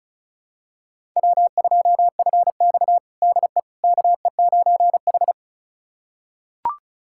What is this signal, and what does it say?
Morse code, heard as a keyed tone, about 36 wpm, 700 Hz. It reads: W2FX DE KE9H
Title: USA Callsign DE Callsign - 4 Characters - 35wpm